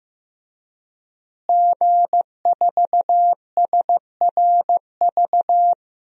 Morse code, heard as a keyed tone, about 15 wpm, 700 Hz. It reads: G4SRV
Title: DXCC-100 Call Signs - 5 Characters - Repeated 3x - 15wpm